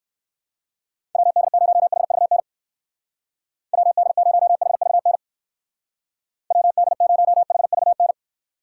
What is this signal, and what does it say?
Morse code, heard as a keyed tone, about 59 wpm, 700 Hz. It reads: WD0HVN WD0HVN WD0HVN